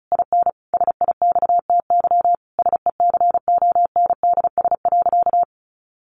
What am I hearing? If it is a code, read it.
Morse code, heard as a keyed tone, about 34 wpm, 700 Hz. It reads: IN SIXTY SECONDS.
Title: A Sound Of Thunder 35wpm